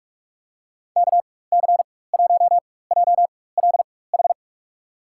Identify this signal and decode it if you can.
Morse code, heard as a keyed tone, about 42 wpm, spaced as for 20 wpm, 700 Hz. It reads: KC1JLH
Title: USA Call Signs - 6 Characters - 4x Character Spacing - 45wpm